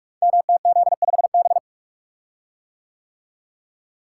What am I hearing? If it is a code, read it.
Morse code, heard as a keyed tone, about 44 wpm, 700 Hz. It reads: MTZ5B
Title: Code Groups - 5 Letter:Numbers - 45wpm